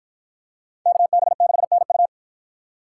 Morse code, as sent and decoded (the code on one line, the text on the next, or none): -.- -... -.... -. ..-
KB6NU